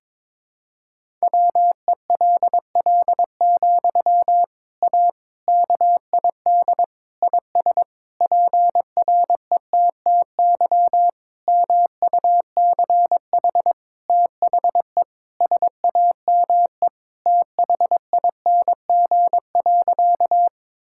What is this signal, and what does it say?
Morse code, heard as a keyed tone, about 22 wpm, 700 Hz. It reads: WELL, A KID IS PRETTY MUCH THE SAME THING.
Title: Sentences from Top 1600 Words 22wpm